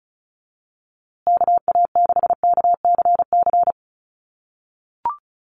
Morse code, heard as a keyed tone, about 34 wpm, 700 Hz. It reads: KA6KCC
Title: USA Call Signs Repeated 2x 35wpm